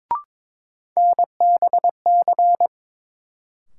Morse code, heard as a keyed tone, about 22 wpm, 700 Hz. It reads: NBC
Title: Sets of 5 Words Encore - New Word Drill - 22wpm